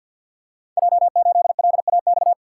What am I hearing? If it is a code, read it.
Morse code, heard as a keyed tone, about 51 wpm, 700 Hz. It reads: J8LAX
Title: Code Groups - 5 Letter:Numbers - 50wpm